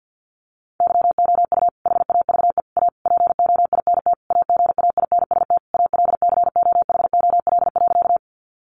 Morse code, heard as a keyed tone, about 50 wpm, 700 Hz. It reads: YOU HAVE A POINT AGAINST ALCOHOL.
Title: Sentences from Top 1400 Words 50wpm